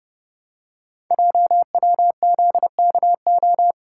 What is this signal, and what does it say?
Morse code, heard as a keyed tone, about 30 wpm, 700 Hz. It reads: JWZKO
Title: Code Groups - 5 Letters - 30wpm